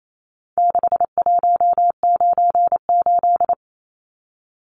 Morse code, heard as a keyed tone, about 28 wpm, 700 Hz. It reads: 6198
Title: Numbers - 4-Digit 28wpm